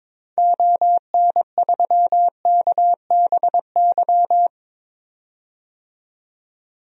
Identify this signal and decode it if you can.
Morse code, heard as a keyed tone, about 21 wpm, 700 Hz. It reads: ON3KBY
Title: DXCC-100 Call Signs - 6 Characters - 22wpm